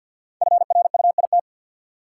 Morse code, heard as a keyed tone, about 52 wpm, 700 Hz. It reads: FRUIT
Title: Sentences from Top 1300 Words - New Word Drill Alt - 50wpm